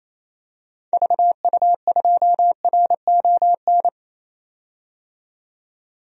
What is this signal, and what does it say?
Morse code, heard as a keyed tone, about 28 wpm, 700 Hz. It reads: VU2RON
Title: DXCC-100 Call Signs - 6 Characters - 28wpm